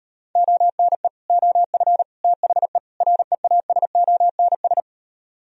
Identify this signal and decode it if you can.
Morse code, heard as a keyed tone, about 38 wpm, 700 Hz. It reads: ONE OF THE REASONS